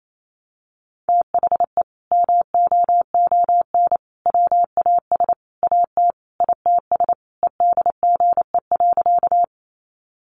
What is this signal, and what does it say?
Morse code, heard as a keyed tone, about 28 wpm, 700 Hz. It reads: THE MOON WAS AT ITS EDGE.
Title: Sentences from Top 1300 Words 28wpm